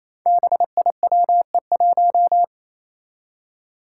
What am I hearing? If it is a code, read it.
Morse code, heard as a keyed tone, about 29 wpm, 700 Hz. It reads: BIWE1
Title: Five Challenging Letter-Number Combinations 28wpm